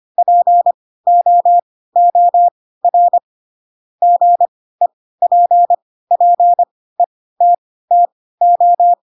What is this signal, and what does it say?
Morse code, heard as a keyed tone, about 25 wpm, spaced as for 16 wpm, 700 Hz. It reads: POOR GEPPETTO
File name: Pinocchio / Pinocchio ch3 10 25 ik1hge